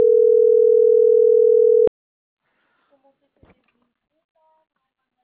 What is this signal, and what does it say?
Ringback tone (North America), 440 + 480 Hz.